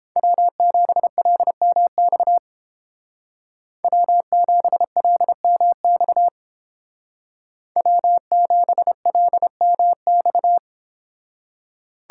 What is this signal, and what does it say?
Morse code, heard as a keyed tone, about 28 wpm, 700 Hz. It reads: W7LMX W7LMX W7LMX